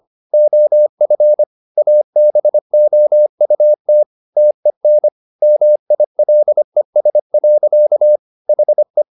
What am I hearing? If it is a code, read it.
Morse code, heard as a keyed tone, about 25 wpm, 600 Hz. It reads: OF ABOUT TEN MILES. HE